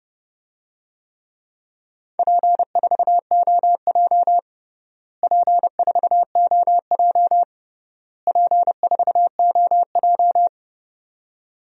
Morse code, heard as a keyed tone, about 30 wpm, 700 Hz. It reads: P4OJ P4OJ P4OJ
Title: DXCC-100 Call Signs - 4 Characters - Repeated 3x - 30wpm